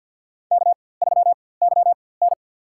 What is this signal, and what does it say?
Morse code, heard as a keyed tone, about 52 wpm, spaced as for 23 wpm, 700 Hz. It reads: K3YN
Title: USA Call Signs - 4x Character Spacing - 50wpm